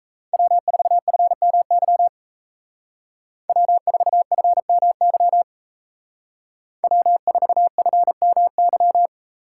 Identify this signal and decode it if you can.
Morse code, heard as a keyed tone, about 37 wpm, 700 Hz. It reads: W4FMY W4FMY W4FMY